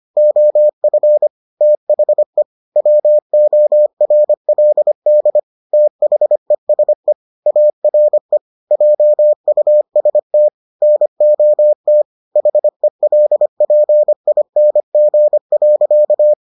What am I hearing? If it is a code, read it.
Morse code, heard as a keyed tone, about 25 wpm, 600 Hz. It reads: OF THE WORLD THESE ARE JUST NOT HELPING.